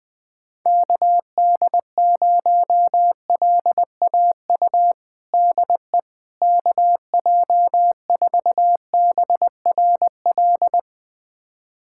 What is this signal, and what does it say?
Morse code, heard as a keyed tone, about 20 wpm, 700 Hz. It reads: KD0LAU DE KJ4BRL
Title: USA Callsign DE Callsign 20wpm